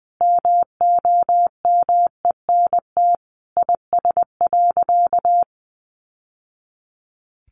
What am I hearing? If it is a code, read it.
Morse code, heard as a keyed tone, about 20 wpm, 700 Hz. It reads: MOMENT IS.
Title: Sentences from Top 1000 Words 20wpm